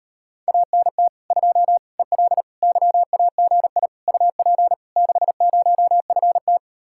Morse code, heard as a keyed tone, about 38 wpm, 700 Hz. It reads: ANT 2 EL YAGI UP 60FT